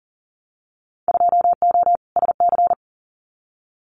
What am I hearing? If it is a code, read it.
Morse code, heard as a keyed tone, about 40 wpm, 700 Hz. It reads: 2O SC
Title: Winter Field Day Exchanges Encore 40wpm